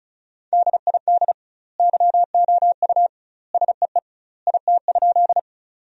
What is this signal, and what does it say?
Morse code, heard as a keyed tone, about 35 wpm, 700 Hz. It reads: DID YOU SEE IT?
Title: 2x Word-Spacing - Sentences from Top 100 Words - 35wpm